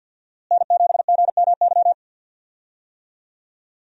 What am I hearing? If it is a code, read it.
Morse code, heard as a keyed tone, about 53 wpm, 690 Hz. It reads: N7GMY
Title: USA Call Signs - 5 Characters - Repeated 3x - 50wpm